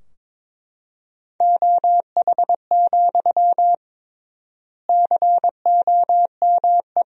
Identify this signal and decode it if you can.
Morse code, heard as a keyed tone, about 22 wpm, 700 Hz. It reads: OH, COME
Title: 3x Word-Spacing - Sentences from Top 500 Words - 22wpm